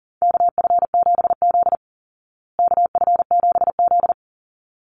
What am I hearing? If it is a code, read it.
Morse code, heard as a keyed tone, about 40 wpm, 700 Hz. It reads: KF7Z KF7Z
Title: Call Signs Repeated 3x - 40wpm